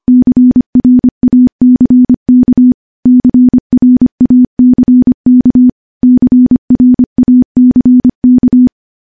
Morse code, heard as a keyed tone, about 25 wpm, 250 Hz. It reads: CRACK CRACK CRACK
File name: morse-training-25WPM